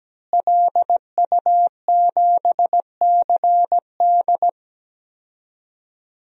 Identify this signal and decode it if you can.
Morse code, heard as a keyed tone, about 17 wpm, 700 Hz. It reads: LU7CD
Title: DXCC-100 Call Signs - 5 Characters - Repeated 3x - 17wpm